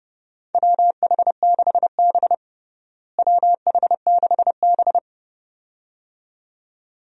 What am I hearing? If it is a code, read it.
Morse code, heard as a keyed tone, about 30 wpm, 700 Hz. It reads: WH6B WH6B
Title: USA Call Signs - 4 Characters - Repeated 2x - 30wpm